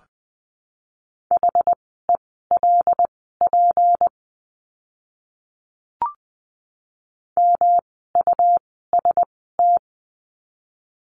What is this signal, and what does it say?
Morse code, heard as a keyed tone, about 20 wpm, spaced as for 14 wpm, 700 Hz. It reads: HELP MUST
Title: 237 Lesson 38f - Letter q - Words 20wpm